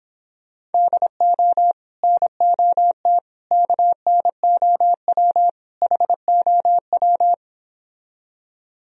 Morse code, heard as a keyed tone, about 29 wpm, 700 Hz. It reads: DO NOT KNOW HOW